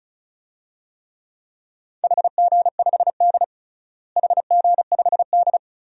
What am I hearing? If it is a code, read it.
Morse code, heard as a keyed tone, about 36 wpm, 700 Hz. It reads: HG5D HG5D
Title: DXCC-100 Call Signs - 4 Characters - Repeated 2x - 35wpm